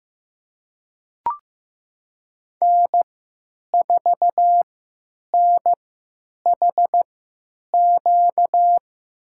Morse code, heard as a keyed tone, about 15 wpm, spaced as for 9 wpm, 700 Hz. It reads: N4NHQ
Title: USA Call Signs - 5 Characters - 3x Character Spacing - 15wpm